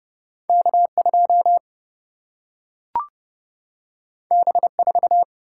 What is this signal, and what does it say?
Morse code, heard as a keyed tone, about 30 wpm, 700 Hz. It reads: K2 B4
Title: Single-Letter Single-Number 30wpm